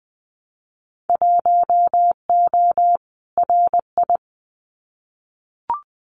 Morse code, heard as a keyed tone, about 20 wpm, 700 Hz. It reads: 1O RI
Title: Winter Field Day Exchanges 20wpm